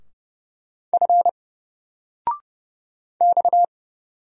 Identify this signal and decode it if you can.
Morse code, heard as a keyed tone, about 30 wpm, 700 Hz. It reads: F X
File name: Single-Letter-Number-30wpm